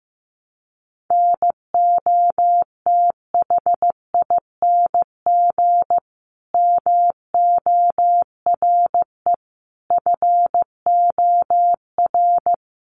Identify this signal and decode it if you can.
Morse code, heard as a keyed tone, about 15 wpm, 700 Hz. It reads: NOTHING MORE FOR
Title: Sentences from Top 1100 Words 15wpm